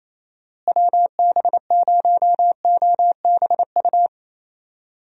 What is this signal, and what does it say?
Morse code, heard as a keyed tone, about 28 wpm, 700 Hz. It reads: WB0OBU
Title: USA Call Signs - 6 Characters - Repeated 3x - 28wpm